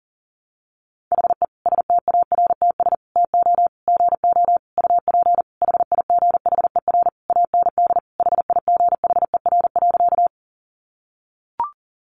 Morse code, heard as a keyed tone, about 40 wpm, 700 Hz. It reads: HE STARTS TO GO UP HIGHER AND HIGHER.
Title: Additional Easy Phrases 40wpm